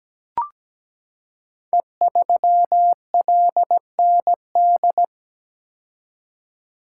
Morse code, heard as a keyed tone, about 17 wpm, 700 Hz. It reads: E3LND
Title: Code Groups - 5 Letter:Numbers - 17wpm